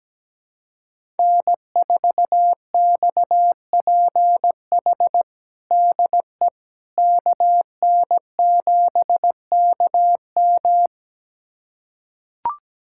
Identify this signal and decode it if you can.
Morse code, heard as a keyed tone, about 17 wpm, 700 Hz. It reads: N4XPH DE KN7KM
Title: USA Callsign DE Callsign - 5 Characters - 17wpm